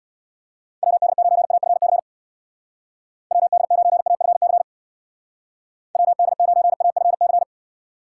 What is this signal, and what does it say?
Morse code, heard as a keyed tone, about 59 wpm, 700 Hz. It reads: WD9AFC WD9AFC WD9AFC
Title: Speed Racing - USA 2x3 Call Signs 50wpm